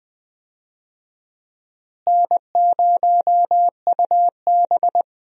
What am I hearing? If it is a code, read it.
Morse code, heard as a keyed tone, about 20 wpm, 700 Hz. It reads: N0UB
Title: USA Call Signs - 4 Characters - Repeated 2x - 20wpm